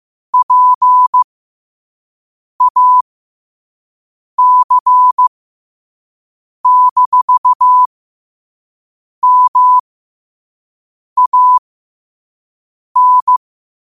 Morse code, heard as a keyed tone, about 15 wpm, spaced as for 5 wpm, 1000 Hz. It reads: PAC-MAN
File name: qotdfiles / 5wpm